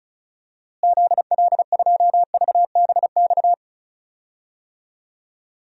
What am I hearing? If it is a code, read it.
Morse code, heard as a keyed tone, about 36 wpm, 700 Hz. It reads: ZL2VBX
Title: DXCC-100 Call Signs - 6 Characters - Repeated 2x - 35wpm